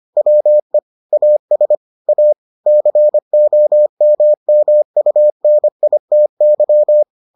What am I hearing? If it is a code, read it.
Morse code, heard as a keyed tone, about 25 wpm, 600 Hz. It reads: WE AS A COMMUNITY